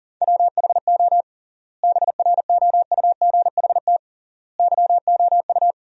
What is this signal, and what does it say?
Morse code, heard as a keyed tone, about 40 wpm, 700 Hz. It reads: WHO BROUGHT YOU
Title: 3x Word-Spacing - Sentences from Top 500 Words - 40wpm